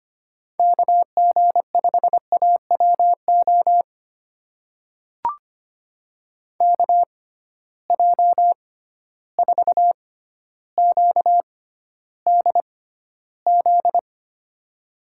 Morse code, heard as a keyed tone, about 25 wpm, spaced as for 13 wpm, 700 Hz. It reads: KG5AWO KJ4QDZ
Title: USA Call Signs - 6 Characters - 6x Character Spacing - 25wpm